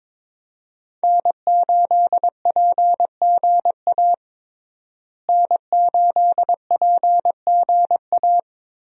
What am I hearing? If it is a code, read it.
Morse code, heard as a keyed tone, about 22 wpm, 700 Hz. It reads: N8PGA N8PGA
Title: USA Call Signs - 5 Characters - Repeated 2x - 22wpm